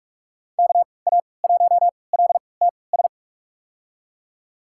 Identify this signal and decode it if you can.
Morse code, heard as a keyed tone, about 46 wpm, spaced as for 25 wpm, 700 Hz. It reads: KA1LTS
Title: USA Call Signs - 3x Character Spacing - 45wpm